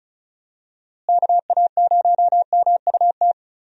Morse code, heard as a keyed tone, about 34 wpm, 700 Hz. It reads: KA0MUT
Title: USA Call Signs - 6 Characters - Repeated 3x - 35wpm